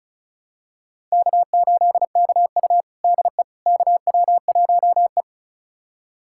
Morse code, heard as a keyed tone, about 36 wpm, 700 Hz. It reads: K8KU DE KW1E
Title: USA Callsign DE Callsign - 4 Characters - 35wpm